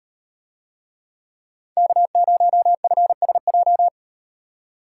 Morse code, heard as a keyed tone, about 38 wpm, 700 Hz. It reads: K0FSJ